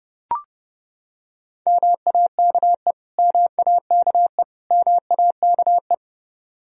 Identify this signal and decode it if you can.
Morse code, heard as a keyed tone, about 32 wpm, 700 Hz. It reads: MAKE MAKE MAKE
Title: Top 100 Words Repeated 30wpm